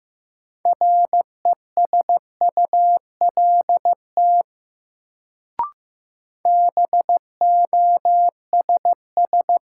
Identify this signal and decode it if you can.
Morse code, heard as a keyed tone, about 15 wpm, 700 Hz. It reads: RESULT BOSS
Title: Sentences from Top 700 Words - New Word Drill - 15wpm